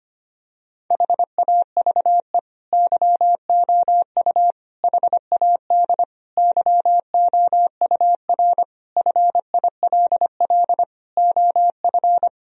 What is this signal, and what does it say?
Morse code, heard as a keyed tone, about 25 wpm, 700 Hz. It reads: HAVE YOU HAD YOUR FILL OF
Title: Sentences from Top 1000 Words - Review Part 3 - 25wpm